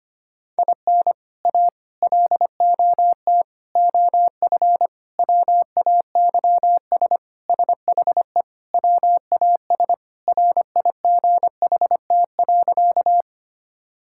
Morse code, heard as a keyed tone, about 25 wpm, 700 Hz. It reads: IN A LOT OF WAYS SHE WAS RIGHT.
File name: practice / 25wpm / Sentences with 4-Word Phrases - 25wpm